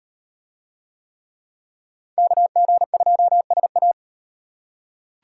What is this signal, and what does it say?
Morse code, heard as a keyed tone, about 38 wpm, 700 Hz. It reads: KG2SA